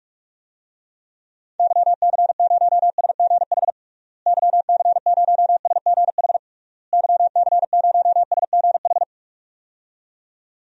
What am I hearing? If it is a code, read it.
Morse code, heard as a keyed tone, about 41 wpm, 700 Hz. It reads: YC0SGH YC0SGH YC0SGH
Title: DXCC-100 Call Signs - 6 Characters - Repeated 3x - 45wpm